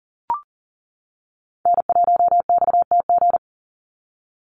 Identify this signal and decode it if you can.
Morse code, heard as a keyed tone, about 38 wpm, 700 Hz. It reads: N1XTG
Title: USA Call Signs 40wpm